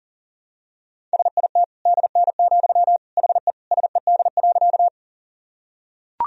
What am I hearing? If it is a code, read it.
Morse code, heard as a keyed tone, about 40 wpm, 700 Hz. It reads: SIT DN, HE SED.
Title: Sentences from Top 500 Words and Abbreviations 40wpm